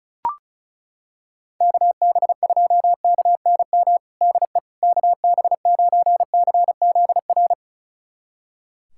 Morse code, heard as a keyed tone, about 36 wpm, 700 Hz. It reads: KB2KNM DE KB9CZR
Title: USA Callsign DE Callsign - 6 Characters - 35wpm